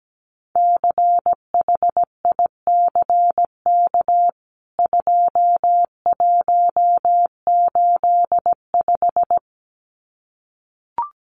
Morse code, heard as a keyed tone, about 17 wpm, 700 Hz. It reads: CHICK 2185
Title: CWT Exchange 17wpm